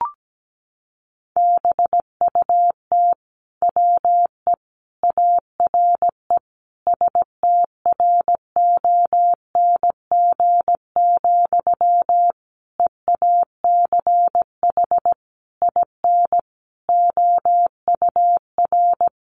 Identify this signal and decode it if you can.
Morse code, heard as a keyed tone, about 17 wpm, 700 Hz. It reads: BUT WE ARE STRONG, EACH IN OUR
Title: Sentences from Top 800 Words 17wpm